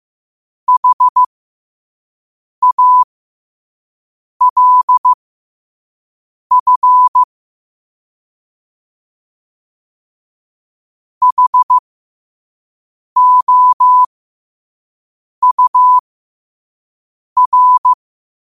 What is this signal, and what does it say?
Morse code, heard as a keyed tone, about 15 wpm, spaced as for 5 wpm, 1000 Hz. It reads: HALF HOUR
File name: qotdfiles / 5wpm